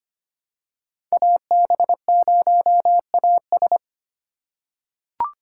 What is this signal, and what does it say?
Morse code, heard as a keyed tone, about 25 wpm, 700 Hz. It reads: AB0AS
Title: USA Call Signs - 5 Characters - 3x Character Spacing - 25wpm